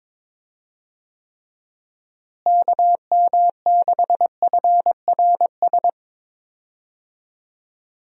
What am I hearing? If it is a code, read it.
Morse code, heard as a keyed tone, about 22 wpm, 700 Hz. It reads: KM6FRS
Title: USA Call Signs - 6 Characters - Repeated 3x - 22wpm